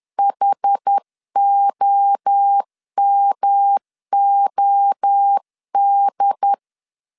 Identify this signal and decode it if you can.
Morse code, heard as a keyed tone, about 10 wpm, 800 Hz. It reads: HOMOD
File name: 10wpm